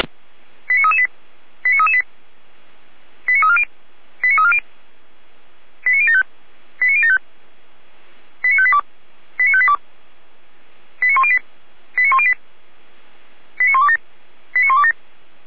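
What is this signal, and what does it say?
Five-tone selective call (ZVEI), heard as quick runs of five tones, tones of about 70 ms. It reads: 89338 89338 89340 89340 89075 89075 89582 89582 89108 89108 89127 89127
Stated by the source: Five-tone ZVEI selective call 89315 received by SDR, fire brigade Feuerwehr Orlamünde.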